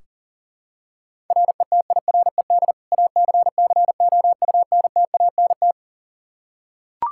Morse code, heard as a keyed tone, about 40 wpm, 690 Hz. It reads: RETIRED ACCOUNTANT
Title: Practice QSO - Exchange 3 - More Info - 40wpm